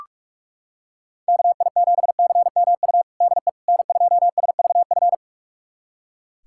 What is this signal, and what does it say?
Morse code, heard as a keyed tone, about 41 wpm, 700 Hz. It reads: KI7CMU DE N2SVF
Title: USA Callsign DE Callsign 45wpm